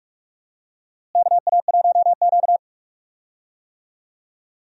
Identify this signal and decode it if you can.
Morse code, heard as a keyed tone, about 43 wpm, 700 Hz. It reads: KA1Q
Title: USA Call Signs - 4 Characters - Repeated 2x - 45wpm